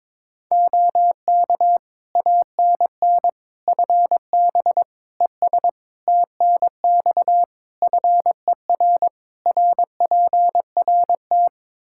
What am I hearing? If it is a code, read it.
Morse code, heard as a keyed tone, about 22 wpm, 700 Hz. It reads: OK ANN FB ES TNX FER RPRT
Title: K1USN SST QSO Practice 22wpm